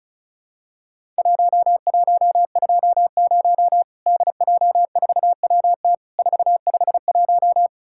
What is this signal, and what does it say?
Morse code, heard as a keyed tone, about 36 wpm, 700 Hz. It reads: 1120 DJ4WT 451
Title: Worked All Europe DX Contest - QTC Exchange - 35wpm